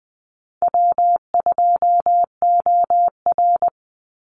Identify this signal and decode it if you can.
Morse code, heard as a keyed tone, about 20 wpm, 700 Hz. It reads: W2OR
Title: Easy Call Signs Repeated 3x - 20wpm